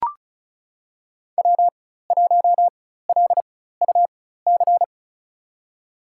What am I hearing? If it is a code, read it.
Morse code, heard as a keyed tone, about 36 wpm, spaced as for 16 wpm, 700 Hz. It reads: W1LUC
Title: USA Call Signs - 5 Characters - 4x Character Spacing - 35wpm